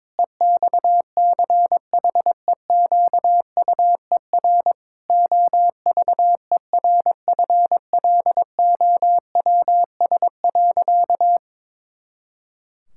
Morse code, heard as a keyed tone, about 22 wpm, 700 Hz. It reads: EXCHEQUER OVERFLOWS.
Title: Pangram Challenge v2 - 22wpm